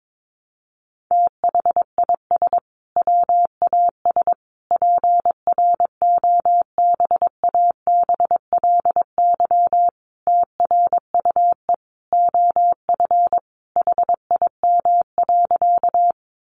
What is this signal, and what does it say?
Morse code, heard as a keyed tone, about 22 wpm, 700 Hz. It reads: THIS WAS PROBABLY TRUE OF HIM.
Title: Sentences from Top 700 Words 22wpm